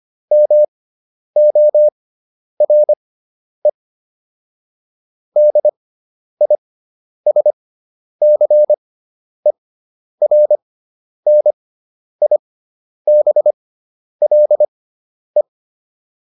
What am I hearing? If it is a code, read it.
Morse code, heard as a keyed tone, about 25 wpm, spaced as for 10 wpm, 600 Hz. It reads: MORE DISCERNIBLE